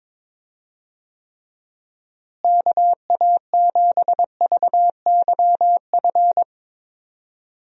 Morse code, heard as a keyed tone, about 22 wpm, 700 Hz. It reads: KA7VYF
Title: USA Call Signs - 6 Characters - Repeated 3x - 22wpm